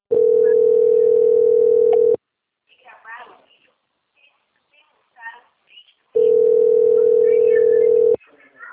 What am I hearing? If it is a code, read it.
Ringback tone (North America), 440 + 480 Hz, with speech.
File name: out- - - - - .18562